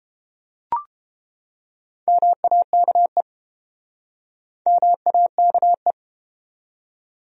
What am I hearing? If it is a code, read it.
Morse code, heard as a keyed tone, about 30 wpm, 700 Hz. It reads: MAKE MAKE